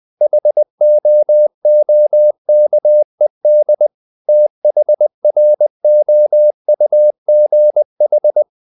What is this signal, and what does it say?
Morse code, heard as a keyed tone, about 20 wpm, 600 Hz. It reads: HOOKED THROUGH
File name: the old man and the sea / the old man and the sea part 1 20wpm